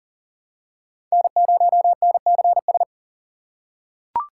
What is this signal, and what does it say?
Morse code, heard as a keyed tone, about 40 wpm, 700 Hz. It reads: N0NCS
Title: USA Call Signs - 5 Characters - 2x Character Spacing - 40wpm